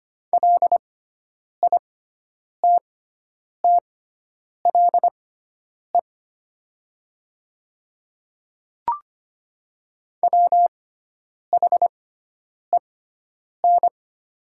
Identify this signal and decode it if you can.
Morse code, heard as a keyed tone, about 25 wpm, spaced as for 9 wpm, 690 Hz. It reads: LITTLE WHEN
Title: 6x Character Spacing - Top 200 Words - 25wpm